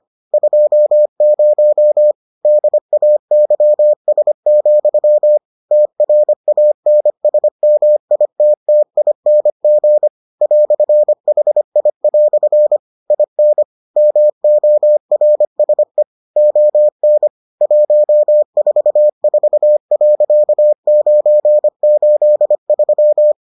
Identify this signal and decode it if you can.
Morse code, heard as a keyed tone, about 25 wpm, 600 Hz. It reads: 20 DAYS, TRANSMITTING "HI" IN MORSE ON 144.983